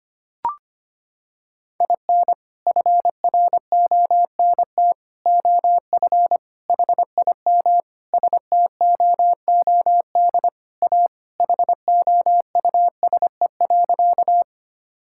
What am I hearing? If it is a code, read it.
Morse code, heard as a keyed tone, about 25 wpm, 700 Hz. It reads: IN FRONT OF HIM STOOD A HOUSE.
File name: practice / 25wpm / Sentences with 4-Word Phrases - 25wpm